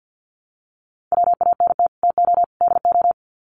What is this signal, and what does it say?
Morse code, heard as a keyed tone, about 50 wpm, 700 Hz. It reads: WANT TO DO
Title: Sets of 3 Words 50wpm